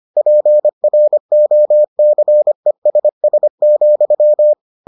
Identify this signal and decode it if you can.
Morse code, heard as a keyed tone, about 25 wpm, 600 Hz. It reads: PROCESS,